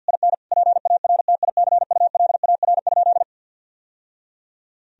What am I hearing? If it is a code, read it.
Morse code, heard as a keyed tone, about 51 wpm, 700 Hz. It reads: IN PARTICULAR?
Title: Sentences from Top 1000 Words - Review Part 4 - 50wpm